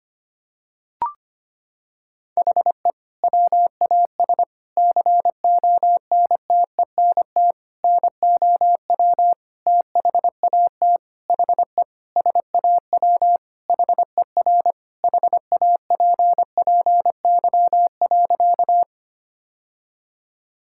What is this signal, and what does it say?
Morse code, heard as a keyed tone, about 25 wpm, 700 Hz. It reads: HE WAS CONTENT NOW THAT HE SAW HER HAPPY.
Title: Sentences from Top 1000 Words - Review Part 6 - 25wpm